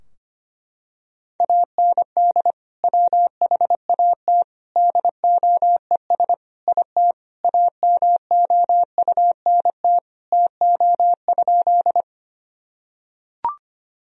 Morse code, heard as a keyed tone, about 25 wpm, 700 Hz. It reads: AND WHAT DOES IT AMOUNT TO?
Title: Sentences from Top 1000 Words - Review Part 4 - 25wpm